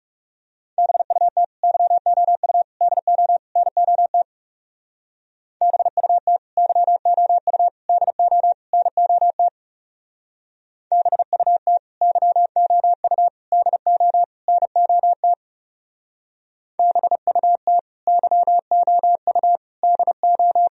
Morse code, heard as a keyed tone, about 37 wpm, 700 Hz. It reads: BUT YOU DO NOT BUT YOU DO NOT BUT YOU DO NOT BUT YOU DO
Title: Speed-Racing - Sets of 4 Words 30wpm